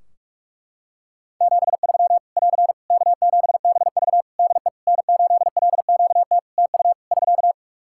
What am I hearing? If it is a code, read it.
Morse code, heard as a keyed tone, about 46 wpm, 700 Hz. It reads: 73 <AR> K7BV DE N8LQT TU <SK>
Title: Practice QSO - Exchange 4 - Ending - 45wpm